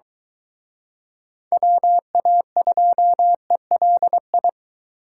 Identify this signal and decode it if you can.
Morse code, heard as a keyed tone, about 22 wpm, 700 Hz. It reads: WA2ELI